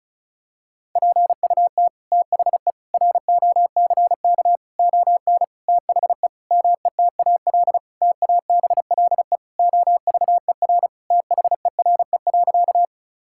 Morse code, heard as a keyed tone, about 36 wpm, 700 Hz. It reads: PUT THE ROCK ON THE METAL TABLE OVER THERE.
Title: Sentences from Top 1000 Words - Review Part 6 - 35wpm